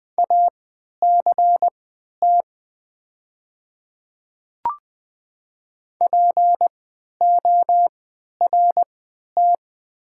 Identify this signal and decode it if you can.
Morse code, heard as a keyed tone, about 20 wpm, spaced as for 11 wpm, 700 Hz. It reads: ACT PORT
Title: 3x Character Spacing - Top 200 Words - 20wpm